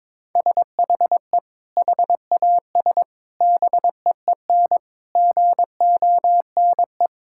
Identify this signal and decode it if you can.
Morse code, heard as a keyed tone, about 21 wpm, 700 Hz. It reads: SHE HAS BEEN GONE
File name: practice / 22wpm / Sentences with 3-Word Phrases - 22wpm